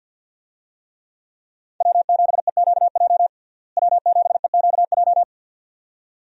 Morse code, heard as a keyed tone, about 51 wpm, 700 Hz. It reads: W7EQJ W7EQJ
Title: USA Call Signs - 5 Characters - Repeated 2x - 50wpm